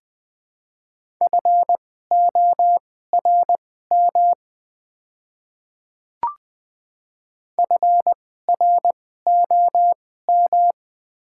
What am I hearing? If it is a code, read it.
Morse code, heard as a keyed tone, about 20 wpm, spaced as for 14 wpm, 700 Hz. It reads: FORM FROM
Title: 2x Character Spacing - Top 200 Words - 20wpm